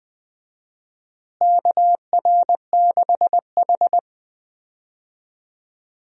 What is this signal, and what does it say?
Morse code, heard as a keyed tone, about 20 wpm, 700 Hz. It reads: KR6H
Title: USA Call Signs - 4 Characters - Repeated 2x - 20wpm